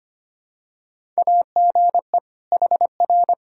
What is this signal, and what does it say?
Morse code, heard as a keyed tone, about 25 wpm, 700 Hz. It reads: AGE HR
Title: Practice QSO - Exchange 3 - More Info - 25wpm